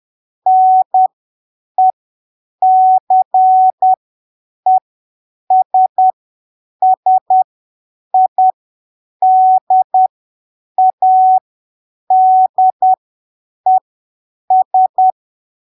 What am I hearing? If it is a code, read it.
Morse code, heard as a keyed tone, about 10 wpm, spaced as for 7 wpm, 750 Hz. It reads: NECESSIDADES
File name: arquivos / texto / texto 23 10ppm